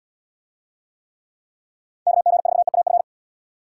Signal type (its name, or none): Morse code